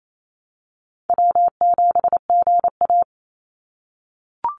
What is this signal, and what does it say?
Morse code, heard as a keyed tone, about 28 wpm, 700 Hz. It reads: W7GA
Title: NA SOTA Chaser Callsigns 28wpm